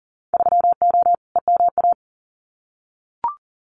Morse code, heard as a keyed tone, about 37 wpm, 700 Hz. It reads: 3O EMA
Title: Winter Field Day Exchanges Encore 40wpm